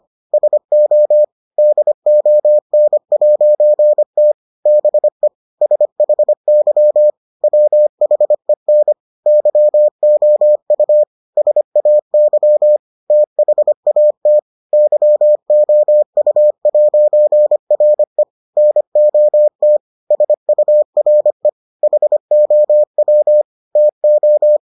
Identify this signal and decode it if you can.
Morse code, heard as a keyed tone, about 25 wpm, 600 Hz. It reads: SO DON'T BE SHY WHEN YOU SAY THAT YOU'RE NOT SURE HOW TO